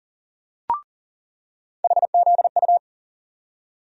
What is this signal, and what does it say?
Morse code, heard as a keyed tone, about 39 wpm, 700 Hz. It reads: HZU
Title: USA Call Sign Postfixes - 3 Letters 40wpm